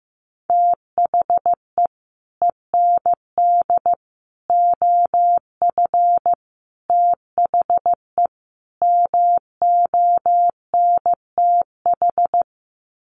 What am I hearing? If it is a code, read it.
Morse code, heard as a keyed tone, about 15 wpm, 700 Hz. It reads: THE END OF THE MONTH
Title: Sets of 5 Words Encore 15wpm